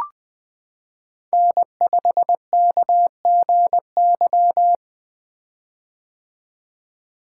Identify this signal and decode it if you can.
Morse code, heard as a keyed tone, about 20 wpm, 700 Hz. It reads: N5KGY
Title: 2023 POTA Hunter Call Signs - 20wpm